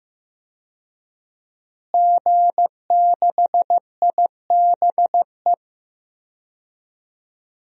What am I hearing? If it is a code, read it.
Morse code, heard as a keyed tone, about 15 wpm, 700 Hz. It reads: G6IBE
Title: DXCC-100 Call Signs - 5 Characters - Repeated 2x - 15wpm